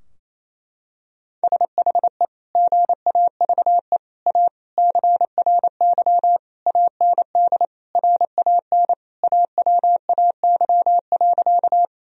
Morse code, heard as a keyed tone, about 27 wpm, 700 Hz. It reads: SHE GAVE A CRY AND RAN AWAY.
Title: Sentences from Top 1000 Words - Review Part 3 - 28wpm